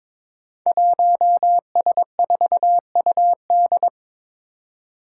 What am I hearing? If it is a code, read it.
Morse code, heard as a keyed tone, about 21 wpm, 700 Hz. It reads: 1S4UD
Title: Five Challenging Letter-Number Combinations 22wpm